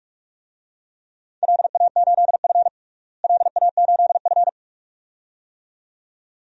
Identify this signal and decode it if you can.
Morse code, heard as a keyed tone, about 46 wpm, 700 Hz. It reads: LA8F LA8F
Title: DXCC-100 Call Signs - 4 Characters - Repeated 2x - 45wpm